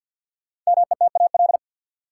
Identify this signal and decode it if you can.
Morse code, heard as a keyed tone, about 50 wpm, 700 Hz. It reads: METAL
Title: Sentences from Top 1600 Words - New Word Drill Alt - 50wpm